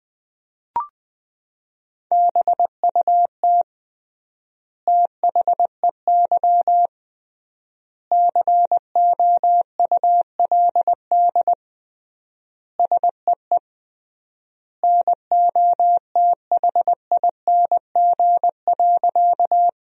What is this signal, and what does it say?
Morse code, heard as a keyed tone, about 20 wpm, 700 Hz. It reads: BUT THEY COULD SEE NOTHING.
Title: 3x Word-Spacing - Sentences from Top 500 Words - 20wpm